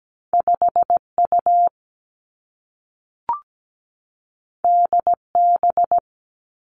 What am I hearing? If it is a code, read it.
Morse code, heard as a keyed tone, about 17 wpm, 700 Hz. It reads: 5 U D B
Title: Two Challenging Letter-Number Combinations 17wpm